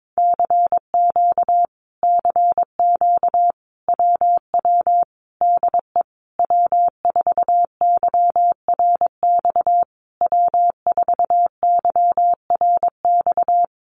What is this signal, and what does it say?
Morse code, heard as a keyed tone, about 22 wpm, 700 Hz. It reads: CQ CQ WW DE W4YRX W4YRX
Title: Common QSO Phrases Encore 22wpm